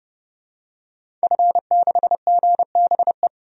Morse code, heard as a keyed tone, about 30 wpm, 700 Hz. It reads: F6GBE
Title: DXCC-100 Call Signs - 5 Characters - 30wpm